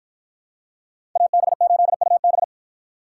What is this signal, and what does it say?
Morse code, heard as a keyed tone, about 53 wpm, 700 Hz. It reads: AB8UB